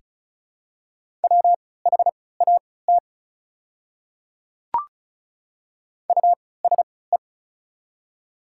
Morse code, heard as a keyed tone, about 34 wpm, spaced as for 20 wpm, 690 Hz. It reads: WHAT USE
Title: 3x Character Spacing - Top 100 Words - 35wpm